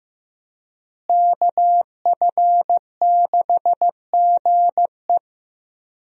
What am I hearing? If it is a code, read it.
Morse code, heard as a keyed tone, about 15 wpm, 700 Hz. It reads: KF6GE
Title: USA Call Signs - 5 Characters - Repeated 2x - 15wpm